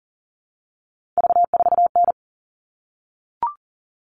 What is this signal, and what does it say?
Morse code, heard as a keyed tone, about 40 wpm, 700 Hz. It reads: V4N
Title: All Cut Numbers - 3-Digit 40wpm